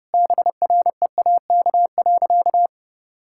Morse code, heard as a keyed tone, about 30 wpm, 700 Hz. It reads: BREAK.
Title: Sentences from Top 1000 Words - Review Part 6 - 30wpm